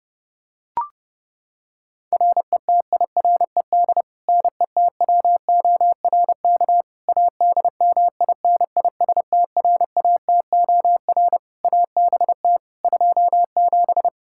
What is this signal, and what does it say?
Morse code, heard as a keyed tone, about 30 wpm, 700 Hz. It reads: RETIRED NETWORK ADMINISTRATOR ABT 27
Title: K1USN SST QSO Practice 30wpm